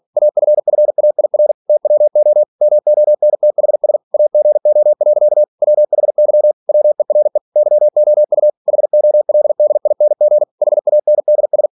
Morse code, heard as a keyed tone, about 46 wpm, 600 Hz. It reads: AFFAIR TWO MONTHS AGO. WHY WERE YOU HOLDING HANDS
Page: Never 47wpm